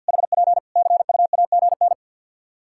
Morse code, heard as a keyed tone, about 50 wpm, 700 Hz. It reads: HP CUAGN
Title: Practice QSO - Exchange 4 - Ending - 50wpm